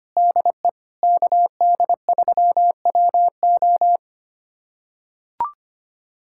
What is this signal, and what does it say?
Morse code, heard as a keyed tone, about 25 wpm, 700 Hz. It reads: DE KD3WO
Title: USA Callsign DE Callsign 25wpm